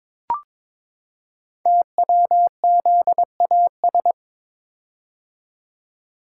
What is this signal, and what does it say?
Morse code, heard as a keyed tone, about 22 wpm, 700 Hz. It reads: TWZAS
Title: Code Groups - 5 Letters - 22wpm